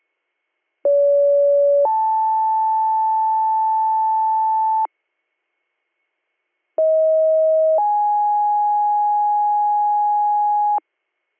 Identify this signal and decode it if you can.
Two-tone page, heard as a one-second tone, then a three-second tone, about 570 then 880 Hz, about 640 then 830 Hz. It is heard from 0.8 to 10.8 s.